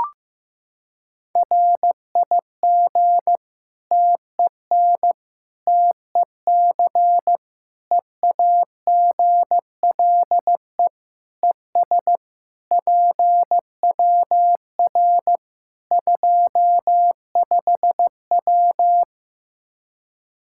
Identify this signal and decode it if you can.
Morse code, heard as a keyed tone, about 15 wpm, 700 Hz. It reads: RIG TEN TEC EAGLE ES PWR 25W
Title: K1USN SST QSO Practice 15wpm